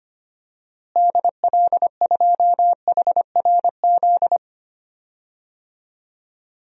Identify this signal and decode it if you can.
Morse code, heard as a keyed tone, about 25 wpm, 700 Hz. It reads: DL2HRZ
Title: DXCC-100 Call Signs - 6 Characters - Repeated 3x - 25wpm